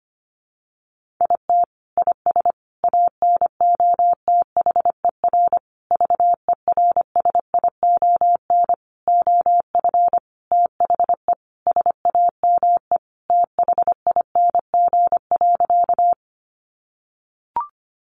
Morse code, heard as a keyed tone, about 25 wpm, 700 Hz. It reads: IT IS ANOTHER VERSION OF THE SAME THING.
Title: Sentences from Top 1400 Words 25wpm